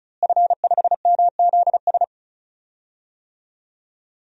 Morse code, heard as a keyed tone, about 36 wpm, 700 Hz. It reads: F5MZS
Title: DXCC-100 Call Signs - 5 Characters - Repeated 3x - 35wpm